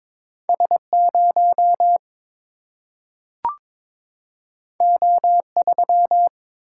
Morse code, heard as a keyed tone, about 22 wpm, 700 Hz. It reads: S 0 O 3
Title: Single-Letter Single-Number 22wpm